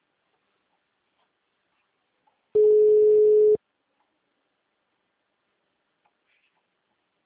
Ringback tone (Europe, 425 Hz).